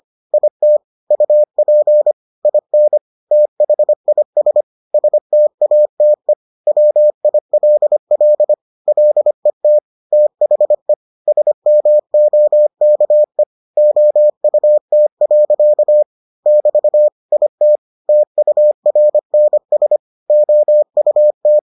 Morse code, heard as a keyed tone, about 25 wpm, 600 Hz. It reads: IT UP IN THIS STATE WILL LET THE SMOKE OUT. <BT> IT TURNS OUT